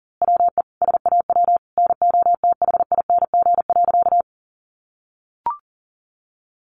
Morse code, heard as a keyed tone, about 40 wpm, 700 Hz. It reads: WE SAW NOTHING.
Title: A Sound Of Thunder 40wpm